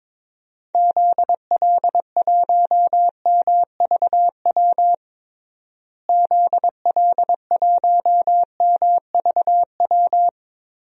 Morse code, heard as a keyed tone, about 22 wpm, 700 Hz. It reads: ZL1MVW ZL1MVW
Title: DXCC-100 Call Signs - 6 Characters - Repeated 3x - 22wpm